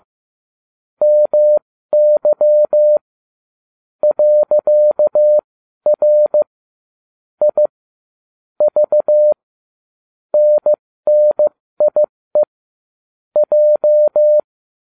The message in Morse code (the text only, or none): MY .R I V NNIE J